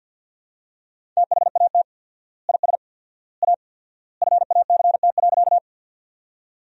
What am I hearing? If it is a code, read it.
Morse code, heard as a keyed tone, about 46 wpm, 700 Hz. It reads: THAT IS A FACT.
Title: 4x Word-Spacing - Sentences from Top 500 Words - 50wpm